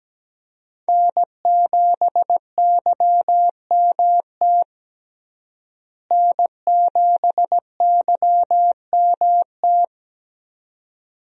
Morse code, heard as a keyed tone, about 17 wpm, 700 Hz. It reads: N7YMT N7YMT
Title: USA Call Signs - 5 Characters - Repeated 2x - 17wpm